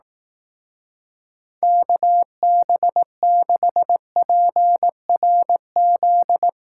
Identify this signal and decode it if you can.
Morse code, heard as a keyed tone, about 19 wpm, 700 Hz. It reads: KB6PRZ
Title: Speed Racing - USA 2x3 Call Signs 15wpm